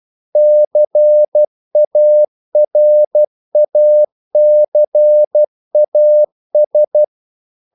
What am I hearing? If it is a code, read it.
Morse code, heard as a keyed tone, about 12 wpm, 600 Hz. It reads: CARACAS
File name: CW / Practice / NationalCapitals12